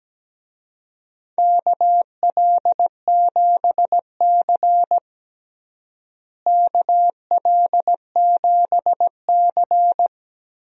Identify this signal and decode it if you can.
Morse code, heard as a keyed tone, about 17 wpm, 700 Hz. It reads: KL7C KL7C
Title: DXCC-100 Call Signs - 4 Characters - Repeated 2x - 17wpm